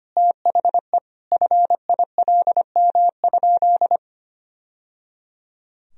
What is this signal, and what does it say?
Morse code, heard as a keyed tone, about 25 wpm, 700 Hz. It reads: THE FILM?
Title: Sentences from Top 1000 Words - Review Part 3 - 25wpm